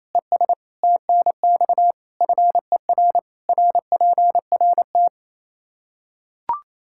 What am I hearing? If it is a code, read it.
Morse code, heard as a keyed tone, about 28 wpm, 700 Hz. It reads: ES TNX FER RPRT
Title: K1USN SST QSO Practice 28wpm